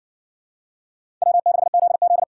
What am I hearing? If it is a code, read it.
Morse code, heard as a keyed tone, about 53 wpm, 700 Hz. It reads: W6ZZ